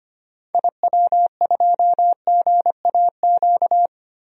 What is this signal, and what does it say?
Morse code, heard as a keyed tone, about 25 wpm, 700 Hz. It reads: IW2GAQ
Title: DXCC-100 Call Signs - 6 Characters - Repeated 3x - 25wpm